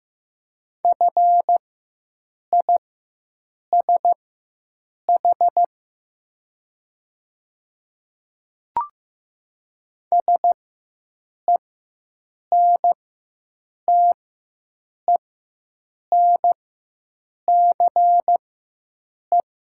Morse code, heard as a keyed tone, about 15 wpm, spaced as for 7 wpm, 700 Hz. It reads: FISH SENTENCE
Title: 4x Character Spacing - Top 300 Words - 15wpm